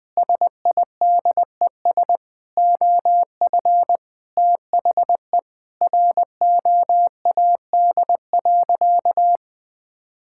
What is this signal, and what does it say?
Morse code, heard as a keyed tone, about 21 wpm, 700 Hz. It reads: SIDES OF THE ROAD.
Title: Sentences with 5-Word Phrases - 20wpm